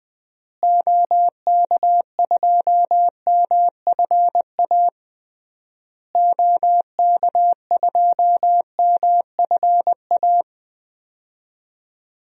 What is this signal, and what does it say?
Morse code, heard as a keyed tone, about 20 wpm, 700 Hz. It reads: OK2MFA OK2MFA
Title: DXCC-100 Call Signs - 6 Characters - Repeated 3x - 20wpm